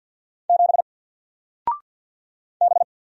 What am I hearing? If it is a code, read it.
Morse code, heard as a keyed tone, about 50 wpm, 700 Hz. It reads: 7 B
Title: Single Character (Letter:Number:Punctuation:Prosign) 50wpm